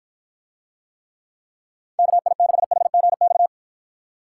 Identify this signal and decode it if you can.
Morse code, heard as a keyed tone, about 53 wpm, 700 Hz. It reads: KI6HGX